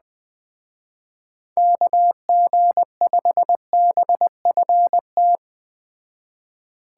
Morse code, heard as a keyed tone, about 20 wpm, 700 Hz. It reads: KG5BFT
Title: Speed Racing - USA 2x3 Call Signs 17wpm